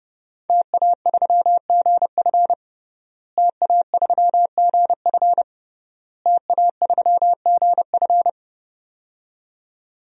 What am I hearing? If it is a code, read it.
Morse code, heard as a keyed tone, about 32 wpm, 700 Hz. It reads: TA3GF TA3GF TA3GF
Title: DXCC-100 Call Signs - 5 Characters - Repeated 3x - 30wpm